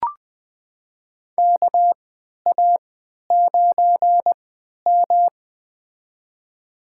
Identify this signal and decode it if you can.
Morse code, heard as a keyed tone, about 21 wpm, spaced as for 11 wpm, 700 Hz. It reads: KA9M
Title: USA Call Signs - 4 Characters - 3x Character Spacing - 20wpm